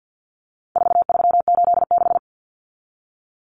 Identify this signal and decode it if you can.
Morse code, heard as a keyed tone, about 48 wpm, 700 Hz. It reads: 4386
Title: Common Cut Numbers - 4-Digit 50wpm